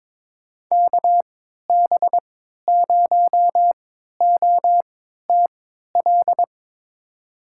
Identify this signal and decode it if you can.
Morse code, heard as a keyed tone, about 22 wpm, spaced as for 13 wpm, 700 Hz. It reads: KB0OTL
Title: USA Call Signs - 6 Characters - 3x Character Spacing - 22wpm